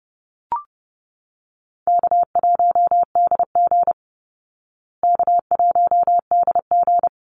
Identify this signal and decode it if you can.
Morse code, heard as a keyed tone, about 30 wpm, 700 Hz. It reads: K1DG K1DG
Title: Easy Call Signs Repeated 2x - 30wpm